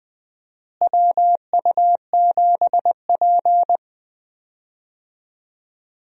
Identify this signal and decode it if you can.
Morse code, heard as a keyed tone, about 20 wpm, 700 Hz. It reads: WU7P
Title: USA Call Signs - 4 Characters - Repeated 2x - 20wpm